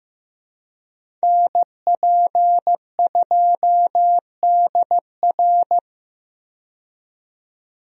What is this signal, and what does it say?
Morse code, heard as a keyed tone, about 15 wpm, 700 Hz. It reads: NP2DR
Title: USA Call Signs - 5 Characters - 15wpm